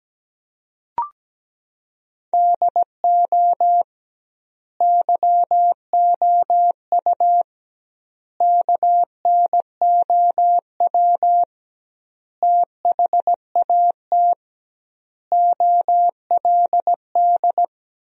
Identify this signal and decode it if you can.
Morse code, heard as a keyed tone, about 17 wpm, 700 Hz. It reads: DO YOU KNOW THAT OLD
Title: 2x Word-Spacing - Sentences from Top 400 Words - 17wpm